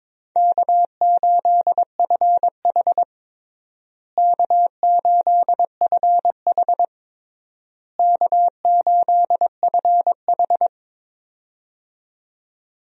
Morse code, heard as a keyed tone, about 22 wpm, 700 Hz. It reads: K8FH K8FH K8FH
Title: USA Call Signs - 4 Characters - Repeated 3x - 22wpm